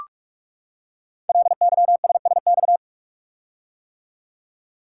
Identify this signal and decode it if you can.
Morse code, heard as a keyed tone, about 44 wpm, 700 Hz. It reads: LYSSX
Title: Code Groups - 5 Letters - 45wpm